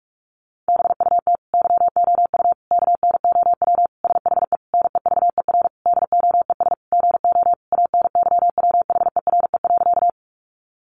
Morse code, heard as a keyed tone, about 44 wpm, 700 Hz. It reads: BUT YOU KNOW SHE NEVER DOES GO ANYWHERE.
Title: Sentences from Top 1400 Words 45wpm